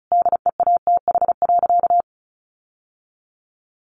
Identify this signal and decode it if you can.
Morse code, heard as a keyed tone, about 34 wpm, 700 Hz. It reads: DEATH.
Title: Sentences from Top 1100 Words 35wpm